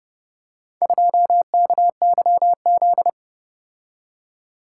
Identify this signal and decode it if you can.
Morse code, heard as a keyed tone, about 29 wpm, 700 Hz. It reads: 2KYZ